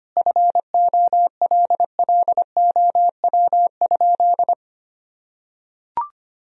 Morse code, heard as a keyed tone, about 25 wpm, 700 Hz. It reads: FOLLOW?
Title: Sentences from Top 1000 Words - Review Part 3 - 25wpm